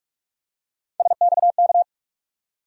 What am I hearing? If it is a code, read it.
Morse code, heard as a keyed tone, about 46 wpm, 700 Hz. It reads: SXK
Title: USA Call Sign Postfixes - 3 Letters 45wpm